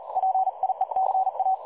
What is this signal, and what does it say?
Morse code, heard as a keyed tone, about 38 wpm, 760 Hz. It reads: PIECA
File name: dlazaawansowanych / nagraniazpasma / websdr recording 3565.0kHz